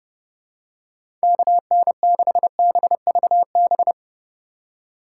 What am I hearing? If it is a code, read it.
Morse code, heard as a keyed tone, about 32 wpm, 700 Hz. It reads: KN6BVB
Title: USA Call Signs - 2x Character Spacing - 30wpm